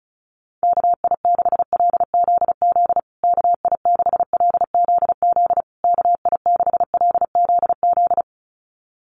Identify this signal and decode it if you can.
Morse code, heard as a keyed tone, about 34 wpm, 700 Hz. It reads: KI6LZZ KI6LZZ KI6LZZ
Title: USA Call Signs Repeated 3x 35wpm